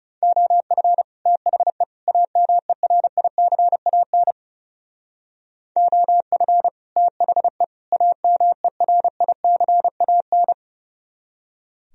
Morse code, heard as a keyed tone, about 37 wpm, 700 Hz. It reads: OF THE AMERICAN OF THE AMERICAN
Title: Speed-Racing - Sets of 3 Words 30wpm